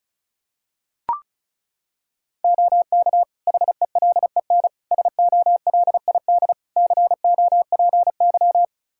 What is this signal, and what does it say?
Morse code, heard as a keyed tone, about 34 wpm, 700 Hz. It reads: OK HELEN SOLID COPY
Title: Practice QSO - Exchange 3 - More Info - 35wpm